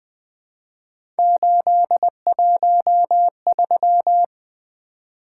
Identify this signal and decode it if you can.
Morse code, heard as a keyed tone, about 21 wpm, 700 Hz. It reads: 813
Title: Three Letter-Number Combinations - 20wpm